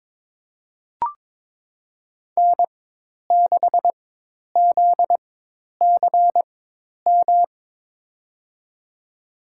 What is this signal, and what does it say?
Morse code, heard as a keyed tone, about 22 wpm, spaced as for 10 wpm, 700 Hz. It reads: N6ZCM
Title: USA Call Signs - 5 Characters - 4x Character Spacing - 22wpm